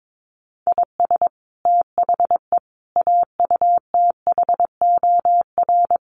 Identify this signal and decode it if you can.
Morse code, heard as a keyed tone, about 22 wpm, 700 Hz. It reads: IS THE AUTHOR
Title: Sets of 5 Words Encore 22wpm